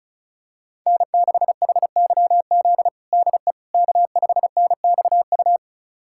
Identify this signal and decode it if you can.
Morse code, heard as a keyed tone, about 36 wpm, 700 Hz. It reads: N6HYZ DE K5NXU
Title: USA Callsign DE Callsign - 5 Characters - 35wpm